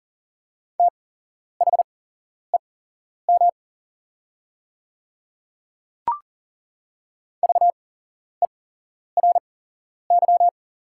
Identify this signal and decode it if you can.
Morse code, heard as a keyed tone, about 40 wpm, spaced as for 11 wpm, 690 Hz. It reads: THEM VERY
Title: 8x Character Spacing - Top 200 Words - 40wpm